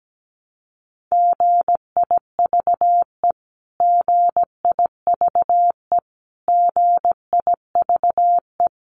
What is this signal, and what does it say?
Morse code, heard as a keyed tone, about 17 wpm, 700 Hz. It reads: GIVE GIVE GIVE
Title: Top 200 Words Repeated 17wpm